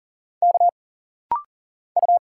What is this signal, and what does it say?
Morse code, heard as a keyed tone, about 40 wpm, 700 Hz. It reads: K U
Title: Single Letters - Rapid-Fire 40wpm